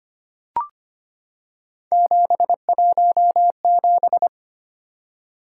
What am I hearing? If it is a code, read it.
Morse code, heard as a keyed tone, about 25 wpm, 700 Hz. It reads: 717
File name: Three Letter-Number Combinations - 25wpm